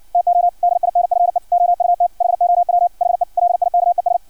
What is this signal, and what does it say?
Morse code, heard as a keyed tone, about 56 wpm, 700 Hz. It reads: TO DINFE OUT HOW HE LIKES
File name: archived / CRSnet